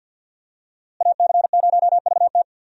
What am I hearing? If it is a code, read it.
Morse code, heard as a keyed tone, about 52 wpm, 700 Hz. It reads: AC0VT
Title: USA Call Signs - 5 Characters - Repeated 2x - 50wpm